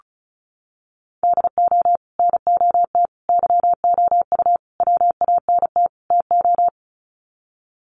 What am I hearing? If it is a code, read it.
Morse code, heard as a keyed tone, about 34 wpm, 700 Hz. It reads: DO NOT YOU WANT TO
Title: Sets of 5 Words Encore2 35wpm